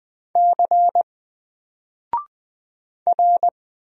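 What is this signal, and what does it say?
Morse code, heard as a keyed tone, about 20 wpm, 700 Hz. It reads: C R
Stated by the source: Single letter and number Morse drill at 20 wpm, from N4LCD.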